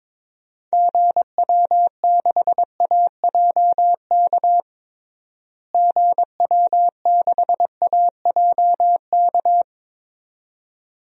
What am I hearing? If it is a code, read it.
Morse code, heard as a keyed tone, about 22 wpm, 700 Hz. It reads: GW6AJK GW6AJK
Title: DXCC-100 Call Signs - 6 Characters - Repeated 2x - 22wpm